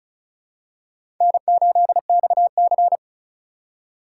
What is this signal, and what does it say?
Morse code, heard as a keyed tone, about 36 wpm, 700 Hz. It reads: N8XC
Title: USA Call Signs - 4 Characters - 4x Character Spacing - 35wpm